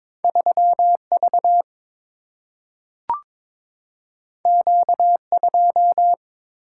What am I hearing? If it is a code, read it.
Morse code, heard as a keyed tone, about 22 wpm, 700 Hz. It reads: 3 V Q 2
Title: Single-Letter Single-Number 22wpm